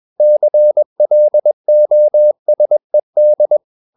Morse code, heard as a keyed tone, about 21 wpm, 600 Hz. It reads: CLOSED
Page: CW med Gnister